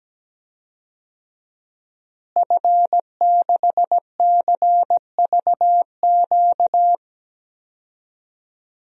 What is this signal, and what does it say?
Morse code, heard as a keyed tone, about 17 wpm, 700 Hz. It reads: F6CVQ
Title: DXCC-100 Call Signs - 5 Characters - Repeated 3x - 17wpm